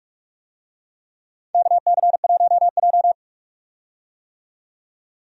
Morse code, heard as a keyed tone, about 46 wpm, 700 Hz. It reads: KC1J
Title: USA Call Signs - 4 Characters - 45wpm